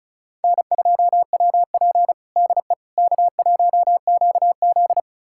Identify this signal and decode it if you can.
Morse code, heard as a keyed tone, about 34 wpm, 700 Hz. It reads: N2WP DE K1QZ
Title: USA Callsign DE Callsign - 4 Characters - 35wpm